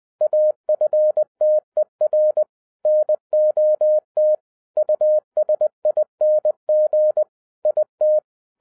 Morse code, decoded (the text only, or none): AFTER NOT USING IT